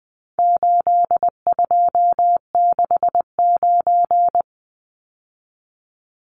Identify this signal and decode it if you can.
Morse code, heard as a keyed tone, about 19 wpm, 700 Hz. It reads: 8269
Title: Numbers - 4-Digit 20wpm